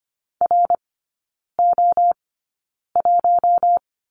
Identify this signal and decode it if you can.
Morse code, heard as a keyed tone, about 25 wpm, spaced as for 9 wpm, 700 Hz. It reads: RO1
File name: course / 25:10wpm / 093 Lesson 16e - Letter c - Three Characters 25wpm